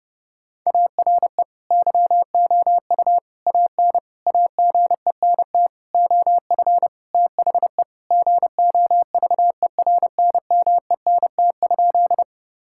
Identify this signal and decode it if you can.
Morse code, heard as a keyed tone, about 32 wpm, 700 Hz. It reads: ARE YOU AN AGENT OF THE GOVERNMENT?
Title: Sentences from Top 1600 Words 30wpm